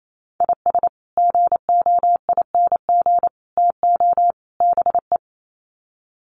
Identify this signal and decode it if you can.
Morse code, heard as a keyed tone, about 28 wpm, 700 Hz. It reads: IS GOING TO BE
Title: Sets of 4 Words 28wpm